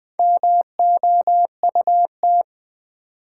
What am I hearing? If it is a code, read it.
Morse code, heard as a keyed tone, about 20 wpm, 700 Hz. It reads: MOUT
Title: Build A Word - 1 Letter At A Time - 5-Letter Words - 20wpm